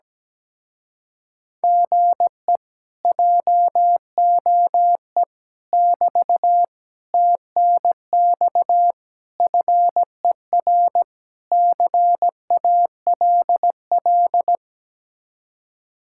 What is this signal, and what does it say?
Morse code, heard as a keyed tone, about 17 wpm, 700 Hz. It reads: GE JOE <BT> TNX FER CALL
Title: Speed-Racing Common QSO Phrases - 17wpm